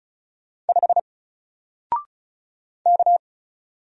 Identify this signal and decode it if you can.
Morse code, heard as a keyed tone, about 36 wpm, 700 Hz. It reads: 5 K
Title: Single Character (Letter:Number:Punctuation:Prosign) Alt 35wpm